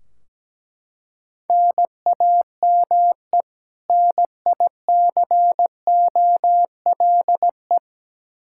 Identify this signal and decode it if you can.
Morse code, heard as a keyed tone, about 17 wpm, 700 Hz. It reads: NAME NICOLE
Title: Practice QSO - Exchange 1 - Beginning - 17wpm